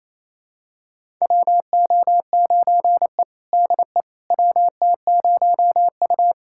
Morse code, heard as a keyed tone, about 28 wpm, 700 Hz. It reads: WO9E DE WT0U
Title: USA Callsign DE Callsign - 4 Characters - 28wpm